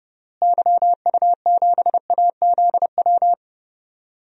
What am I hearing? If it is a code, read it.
Morse code, heard as a keyed tone, about 32 wpm, 700 Hz. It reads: YU7AZW
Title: DXCC-100 Call Signs - 6 Characters - Repeated 3x - 30wpm